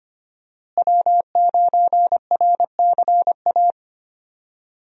Morse code, heard as a keyed tone, about 25 wpm, 700 Hz. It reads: W9RCA
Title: USA Call Signs - 5 Characters - Repeated 2x - 25wpm